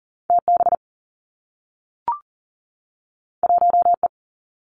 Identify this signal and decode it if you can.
Morse code, heard as a keyed tone, about 42 wpm, 700 Hz. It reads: T B 1 E
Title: Two Challenging Letter-Number Combinations 40wpm